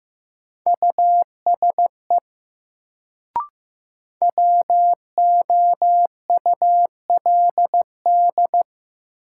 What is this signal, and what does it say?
Morse code, heard as a keyed tone, about 15 wpm, 700 Hz. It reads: USE WOULD
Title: Sentences from Top 100 Words - New Word Drill - 15wpm